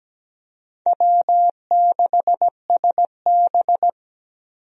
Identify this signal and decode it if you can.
Morse code, heard as a keyed tone, about 17 wpm, 700 Hz. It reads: W6SB
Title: USA Call Signs - 4 Characters - 17wpm